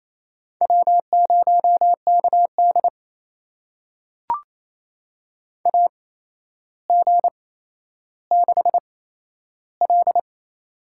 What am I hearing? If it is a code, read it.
Morse code, heard as a keyed tone, about 28 wpm, spaced as for 12 wpm, 700 Hz. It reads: W0KD AG6L
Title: USA Call Signs - 8x Character Spacing - 28wpm